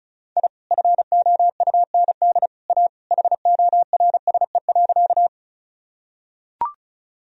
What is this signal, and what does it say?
Morse code, heard as a keyed tone, about 34 wpm, 700 Hz. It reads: I FOUND A HORSE.
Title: Sentences from Top 1000 Words - Review Part 1 - 35wpm